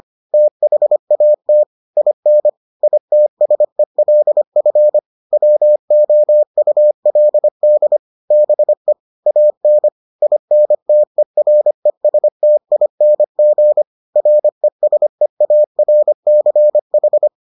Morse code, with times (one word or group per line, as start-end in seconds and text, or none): 0.3-1.6 THAT
2.0-2.5 IN
2.8-5.0 ITSELF
5.3-8.0 WOULD
8.3-8.9 BE
9.3-9.9 AN
10.2-13.8 INTERESTING
14.2-17.3 RESEARCH